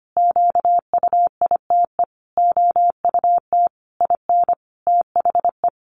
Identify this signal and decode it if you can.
Morse code, heard as a keyed tone, about 25 wpm, 700 Hz. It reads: QUITE OUT IN THE
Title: Sentences from Top 700 Words 25wpm